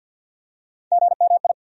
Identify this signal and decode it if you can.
Morse code, heard as a keyed tone, about 52 wpm, 700 Hz. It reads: GMI
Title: USA Call Sign Postfixes - 3 Letters 50wpm